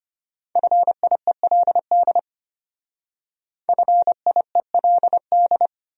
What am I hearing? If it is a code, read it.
Morse code, heard as a keyed tone, about 31 wpm, 700 Hz. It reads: FIELD FIELD
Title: Speed-Racing Top 500 Words 25wpm